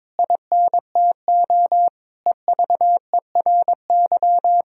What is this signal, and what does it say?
Morse code, heard as a keyed tone, about 21 wpm, 700 Hz. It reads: INTO EVERY
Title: Sentences from Top 1000 Words - Review Part 3 - 22wpm